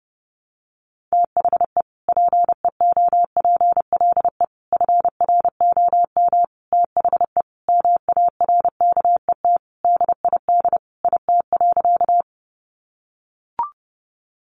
Morse code, heard as a keyed tone, about 30 wpm, 700 Hz. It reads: THE PEOPLE FROM THE MARKET DID IT.
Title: Sentences from Top 600 Words 30wpm